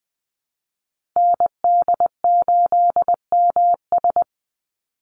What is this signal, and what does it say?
Morse code, heard as a keyed tone, about 20 wpm, 700 Hz. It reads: ND8MS
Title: Call Signs Repeated 3x - 20wpm